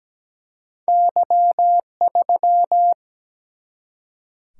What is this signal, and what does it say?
Morse code, heard as a keyed tone, about 17 wpm, 700 Hz. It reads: Y3
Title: Single-Letter Single-Number 17wpm